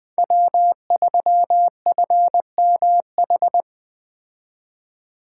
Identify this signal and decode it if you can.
Morse code, heard as a keyed tone, about 20 wpm, 700 Hz. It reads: W3FMH
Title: Speed Racing - USA 1x3 Call Signs 17wpm